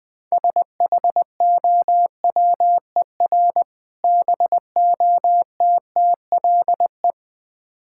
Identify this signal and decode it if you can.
Morse code, heard as a keyed tone, about 20 wpm, 700 Hz. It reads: SHOWER BOTTLE
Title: Easy Words - Speed-Racing with Hints - 20wpm